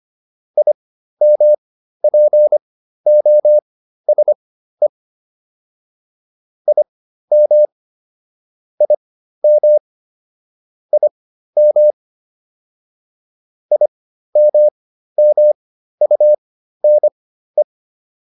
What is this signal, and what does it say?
Morse code, heard as a keyed tone, about 25 wpm, spaced as for 13 wpm, 600 Hz. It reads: IMPOSE IM IM IM IMMUNE